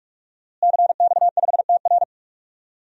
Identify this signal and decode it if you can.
Morse code, heard as a keyed tone, about 46 wpm, 700 Hz. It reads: CX5TR
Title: DXCC-100 Call Signs - 5 Characters - Repeated 3x - 45wpm